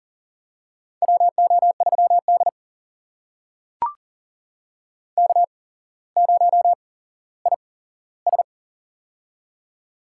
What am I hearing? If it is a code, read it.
Morse code, heard as a keyed tone, about 37 wpm, spaced as for 17 wpm, 700 Hz. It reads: WO3D K0IS
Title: USA Call Signs - 4 Characters - 8x Character Spacing - 40wpm